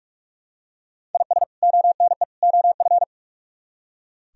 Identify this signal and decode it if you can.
Morse code, heard as a keyed tone, about 45 wpm, 700 Hz. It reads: IS ONE OF
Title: Speed-Racing - Sets of 3 Words 45wpm